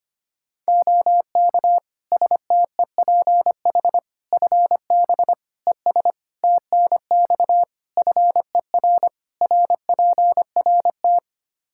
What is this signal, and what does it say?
Morse code, heard as a keyed tone, about 25 wpm, 700 Hz. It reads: OK STEPH FB ES TNX FER RPRT
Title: Practice QSO - Exchange 2 - Info - 25wpm